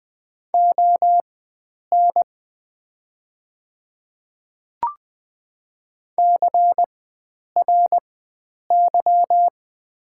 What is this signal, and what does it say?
Morse code, heard as a keyed tone, about 20 wpm, spaced as for 9 wpm, 700 Hz. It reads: ON CRY
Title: 4x Character Spacing - Top 500 Words - 20wpm